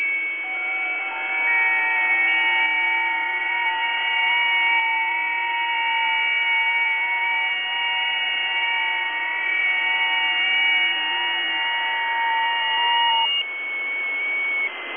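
FT8.
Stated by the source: FT8 signal received through a WebSDR.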